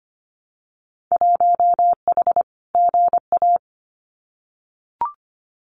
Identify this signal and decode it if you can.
Morse code, heard as a keyed tone, about 25 wpm, 700 Hz. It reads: 1H GA
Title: Winter Field Day Exchanges Encore 25wpm